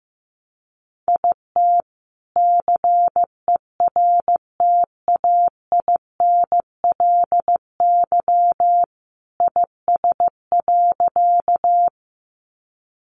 Morse code, heard as a keyed tone, about 15 wpm, 700 Hz. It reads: IT CERTAINLY IS.
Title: Sentences from Top 900 Words 15wpm